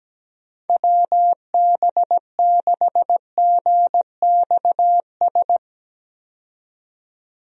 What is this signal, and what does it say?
Morse code, heard as a keyed tone, about 17 wpm, 700 Hz. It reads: WB6GXS
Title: USA Call Signs - 6 Characters - 17wpm